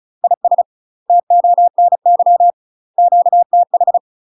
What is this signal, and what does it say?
Morse code, heard as a keyed tone, about 34 wpm, 700 Hz. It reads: IS TONY QTH
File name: UK MASTER QSOs TRAINING / 35WPM / UK MASTER QSOs 0004